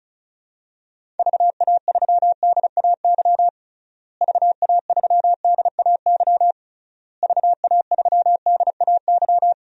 Morse code, heard as a keyed tone, about 36 wpm, 700 Hz. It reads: VA3DAY VA3DAY VA3DAY
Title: DXCC-100 Call Signs - 6 Characters - Repeated 3x - 35wpm